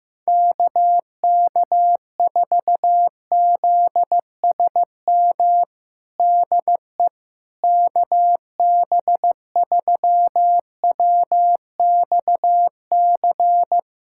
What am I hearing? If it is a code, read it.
Morse code, heard as a keyed tone, about 15 wpm, 700 Hz. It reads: KK4ZSM DE KB3WXC
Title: USA Callsign DE Callsign - 6 Characters - 15wpm